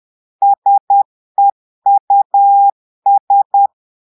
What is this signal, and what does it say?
Morse code, heard as a keyed tone, about 10 wpm, 800 Hz. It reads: SEUS
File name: portugais texte en clair 2 vitesse 10 mots minutes